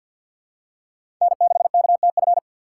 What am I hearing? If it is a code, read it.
Morse code, heard as a keyed tone, about 51 wpm, 700 Hz. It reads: N6KTF
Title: USA Call Signs - 5 Characters - 3x Character Spacing - 50wpm